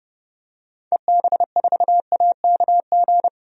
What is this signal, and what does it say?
Morse code, heard as a keyed tone, about 30 wpm, 690 Hz. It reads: EB4AKG
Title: DXCC-100 Call Signs - 6 Characters - 30wpm